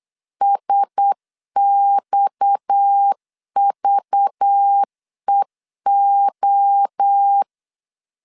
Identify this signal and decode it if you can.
Morse code, heard as a keyed tone, about 8 wpm, 800 Hz. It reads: SXVEO
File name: LEZIONI / 7 GF9Z / 7 GF9 8wpm 3545